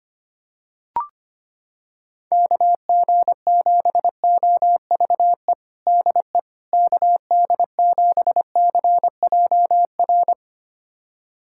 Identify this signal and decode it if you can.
Morse code, heard as a keyed tone, about 25 wpm, 700 Hz. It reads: KG7OVE DE KD7CJR
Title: USA Callsign DE Callsign 25wpm